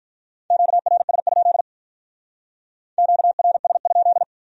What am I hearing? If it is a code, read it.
Morse code, heard as a keyed tone, about 44 wpm, 700 Hz. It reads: QRS? QRS?